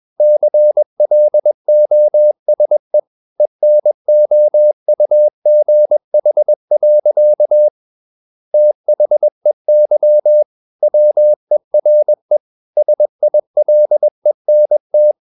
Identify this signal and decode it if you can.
Morse code, heard as a keyed tone, about 21 wpm, 600 Hz. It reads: CLOSE ENOUGH. THEY WERE SILENT
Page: CW med Gnister